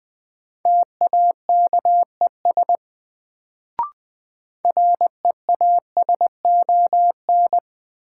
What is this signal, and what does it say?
Morse code, heard as a keyed tone, about 20 wpm, 700 Hz. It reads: TAKES REASON
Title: Sets of 5 Words Encore - New Word Drill - 20wpm